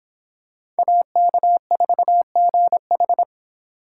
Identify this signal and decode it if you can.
Morse code, heard as a keyed tone, about 25 wpm, 700 Hz. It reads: AK4GH